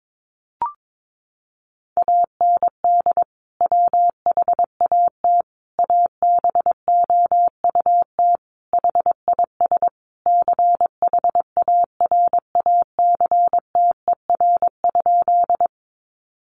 Morse code, heard as a keyed tone, about 22 wpm, 700 Hz. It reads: AND WHAT ABOUT HIS CHARACTER?
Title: Sentences from Top 1200 Words 22wpm